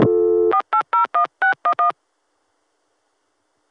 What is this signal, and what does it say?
DTMF, heard as telephone touch tones, tones of about 100 ms. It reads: 4801611